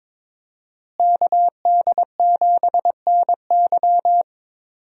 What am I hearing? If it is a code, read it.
Morse code, heard as a keyed tone, about 22 wpm, 700 Hz. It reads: KD7NY
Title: USA Call Signs - 5 Characters - Repeated 3x - 22wpm